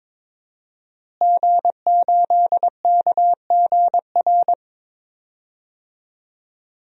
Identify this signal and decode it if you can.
Morse code, heard as a keyed tone, about 21 wpm, 700 Hz. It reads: G8KGR
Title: DXCC-100 Call Signs - 5 Characters - Repeated 2x - 22wpm